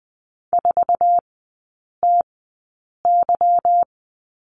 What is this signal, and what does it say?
Morse code, heard as a keyed tone, about 20 wpm, spaced as for 8 wpm, 690 Hz. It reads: 4TY
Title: 145 Lesson 24e - Letter y - Three Characters 20wpm